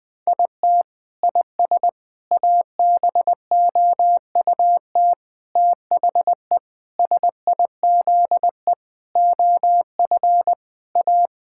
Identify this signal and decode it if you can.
Morse code, heard as a keyed tone, about 21 wpm, 700 Hz. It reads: IT IS ABOUT THE SIZE OF A
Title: Sentences with 5-Word Phrases - 20wpm